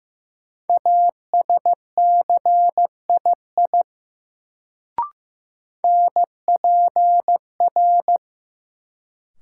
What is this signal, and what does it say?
Morse code, heard as a keyed tone, about 15 wpm, 700 Hz. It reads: ASCII NPR
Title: Sets of 5 Words - New Word Drill - 15wpm